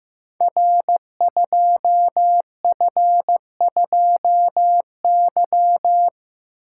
Morse code, heard as a keyed tone, about 15 wpm, 700 Hz. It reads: R2F2Y
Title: Code Groups - 5 Letter:Numbers - 15wpm